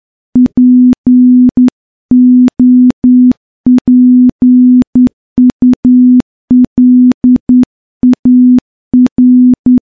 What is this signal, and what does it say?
Morse code, heard as a keyed tone, about 10 wpm, 250 Hz. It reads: POPULAR
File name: morse-training-10WPM